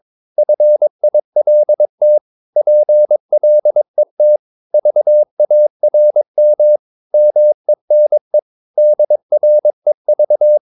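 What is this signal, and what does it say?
Morse code, heard as a keyed tone, about 22 wpm, 600 Hz. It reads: FILT PLET VARM MENE DREV
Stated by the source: Morse practice of short Danish words at 22 wpm, from CW med Gnister.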